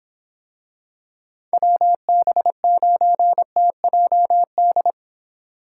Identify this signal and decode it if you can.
Morse code, heard as a keyed tone, about 26 wpm, 700 Hz. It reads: WB9TJD